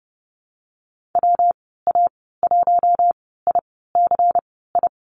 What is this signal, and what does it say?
Morse code, heard as a keyed tone, about 30 wpm, spaced as for 17 wpm, 700 Hz. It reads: WA1ICI
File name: Lesson 19g - Letter w - Call Signs 30wpm